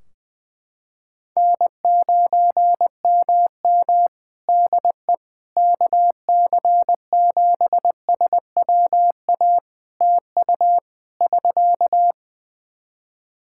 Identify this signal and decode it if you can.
Morse code, heard as a keyed tone, about 20 wpm, 700 Hz. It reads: N9MM DE KC7SWA TU <SK>
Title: Practice QSO - Exchange 4 - Ending - 20wpm